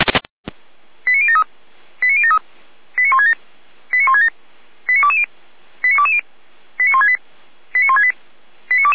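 Five-tone selective call (ZVEI), heard as quick runs of five tones, tones of about 70 ms. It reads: 89073 89073 89167 89167 89220 89220 89168 89168
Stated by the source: ZVEI five-tone call received by SDR, feuerwehr-orlamuende.org fire brigade audio archive, code 89022.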